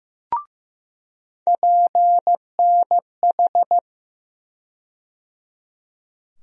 Morse code, heard as a keyed tone, about 15 wpm, 700 Hz. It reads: PNH
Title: USA Call Sign Postfixes - 3 Letters 15wpm